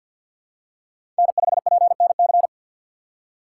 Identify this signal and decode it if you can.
Morse code, heard as a keyed tone, about 51 wpm, 700 Hz. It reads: N5PNC
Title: USA Call Signs - 3x Character Spacing - 50wpm